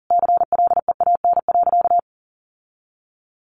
Morse code, heard as a keyed tone, about 40 wpm, 700 Hz. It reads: CLEAN.
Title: Sentences from Top 1100 Words 40wpm